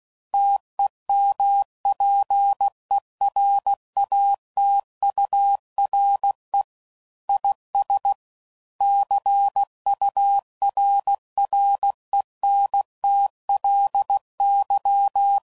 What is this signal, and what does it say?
Morse code, heard as a keyed tone, about 16 wpm, 800 Hz. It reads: TEMPERATURE IS CURRENTLY
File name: CW Code Groups @15WPM / Weather Sets / Weather 4 28